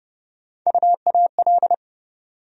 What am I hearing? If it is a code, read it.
Morse code, heard as a keyed tone, about 30 wpm, 700 Hz. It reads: UAL
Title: USA Call Sign Postfixes - 3 Letters 30wpm